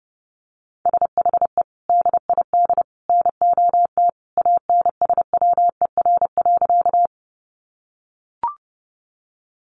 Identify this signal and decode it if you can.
Morse code, heard as a keyed tone, about 30 wpm, 700 Hz. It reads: SHE DID NOT ANSWER.
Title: Sentences from Top 600 Words 30wpm